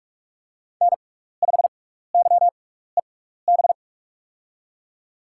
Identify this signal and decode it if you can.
Morse code, heard as a keyed tone, about 44 wpm, spaced as for 15 wpm, 700 Hz. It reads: N5YEB
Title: USA Call Signs - 5 Characters - 6x Character Spacing - 45wpm